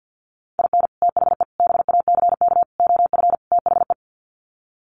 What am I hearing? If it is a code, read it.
Morse code, heard as a keyed tone, about 50 wpm, 700 Hz. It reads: IN THE BACK OF THE
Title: Sets of 5 Words 50wpm